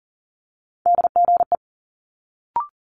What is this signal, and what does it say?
Morse code, heard as a keyed tone, about 40 wpm, 700 Hz. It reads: DGE
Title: Multi-letter Phonemes Quick Fire 40wpm